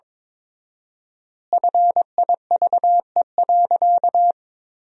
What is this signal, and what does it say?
Morse code, heard as a keyed tone, about 23 wpm, 700 Hz. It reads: FIVE.
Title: 4x Word-Spacing - Sentences from Top 400 Words - 22wpm